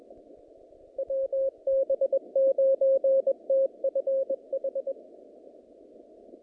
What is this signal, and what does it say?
Morse code, heard as a keyed tone, about 22 wpm, 540 Hz. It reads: WB9TFH
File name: contests / naqp-cw-k5tr-op-k5pi- -2200